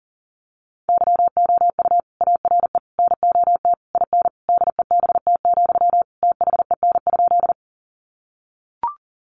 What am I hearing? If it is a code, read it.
Morse code, heard as a keyed tone, about 40 wpm, 700 Hz. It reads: YOU ARE NOT IN DEBT, THEN?
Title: Sentences from Top 1300 Words 40wpm